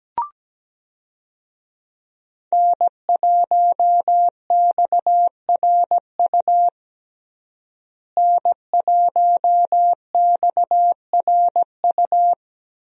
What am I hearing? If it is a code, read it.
Morse code, heard as a keyed tone, about 17 wpm, 700 Hz. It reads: N1XRU N1XRU
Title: USA Call Signs - 5 Characters - Repeated 2x - 17wpm